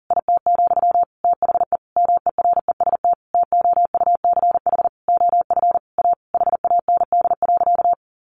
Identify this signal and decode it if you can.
Morse code, heard as a keyed tone, about 40 wpm, 700 Hz. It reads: IT, THE MEREST TOUCH OF A HAND.
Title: A Sound Of Thunder 40wpm